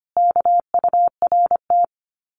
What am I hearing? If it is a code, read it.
Morse code, heard as a keyed tone, about 25 wpm, 700 Hz. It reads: KURT
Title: Common QSO Phrases Encore 25wpm